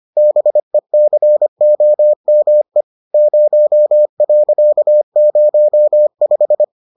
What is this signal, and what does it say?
Morse code, heard as a keyed tone, about 25 wpm, 600 Hz. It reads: BECOME 0.05